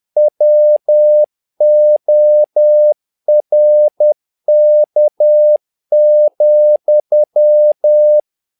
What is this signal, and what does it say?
Morse code, heard as a keyed tone, about 10 wpm, 600 Hz. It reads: WORK,
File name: learning center / tower trouble / tower trouble 10wpm